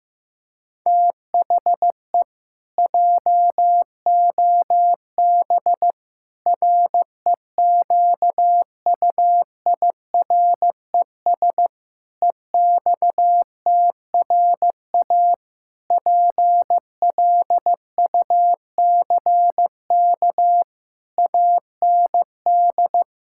Morse code, heard as a keyed tone, about 16 wpm, 700 Hz. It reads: THE JOB REQUIRES EXTRA PLUCK AND
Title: Pangram Challenge v2 - 15wpm